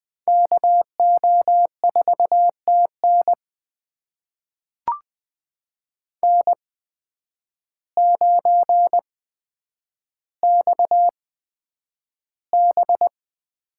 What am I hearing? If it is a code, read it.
Morse code, heard as a keyed tone, about 20 wpm, spaced as for 9 wpm, 700 Hz. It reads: KO4TN N9XB
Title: USA Call Signs - 5 Characters - 8x Character Spacing - 20wpm